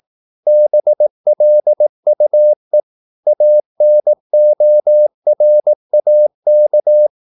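Morse code, heard as a keyed tone, about 18 wpm, 600 Hz. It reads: BLUE ANORAK
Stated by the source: Morse practice at 18 wpm, English text, from CW med Gnister.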